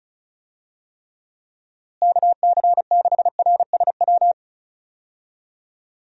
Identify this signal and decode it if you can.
Morse code, heard as a keyed tone, about 34 wpm, 700 Hz. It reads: KC6RSW
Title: USA Call Signs - 6 Characters - Repeated 3x - 35wpm